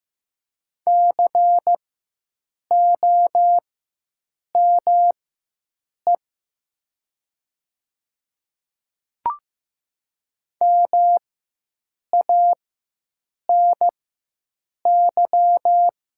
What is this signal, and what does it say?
Morse code, heard as a keyed tone, about 15 wpm, spaced as for 7 wpm, 700 Hz. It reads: COME MANY
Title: 4x Character Spacing - Top 100 Words - 15wpm